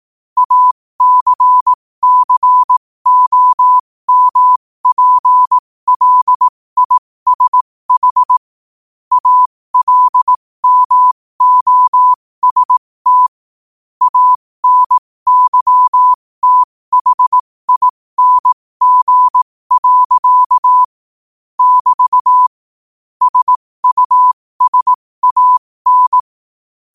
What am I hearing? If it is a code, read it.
Morse code, heard as a keyed tone, about 18 wpm, spaced as for 15 wpm, 1000 Hz. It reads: ACCOMPLISH ALMOST ANYTHING. <BT> SUSAN
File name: qotdfiles / 15wpm